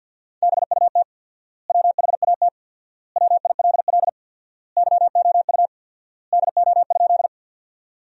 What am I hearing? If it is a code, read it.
Morse code, heard as a keyed tone, about 50 wpm, 700 Hz. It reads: BUT WHAT WILL YOU DO?
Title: 4x Word-Spacing - Sentences from Top 100 Words - 50wpm